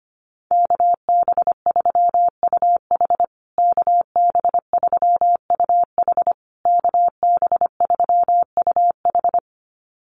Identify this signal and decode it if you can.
Morse code, heard as a keyed tone, about 25 wpm, 700 Hz. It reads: KB3UH KB3UH KB3UH
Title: USA Call Signs Repeated 3x 25wpm